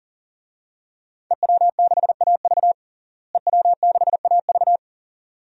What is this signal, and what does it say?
Morse code, heard as a keyed tone, about 40 wpm, 700 Hz. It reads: EW6AV EW6AV
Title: DXCC-100 Call Signs - 5 Characters - Repeated 2x - 40wpm